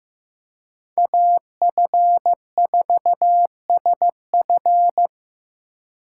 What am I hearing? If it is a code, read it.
Morse code, heard as a keyed tone, about 15 wpm, 700 Hz. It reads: AF4SF
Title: USA Call Signs - 2x Character Spacing - 15wpm